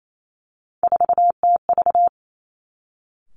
Morse code, heard as a keyed tone, about 28 wpm, 690 Hz. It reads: 4TV
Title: Three Challenging Letter-Number Combinations 28wpm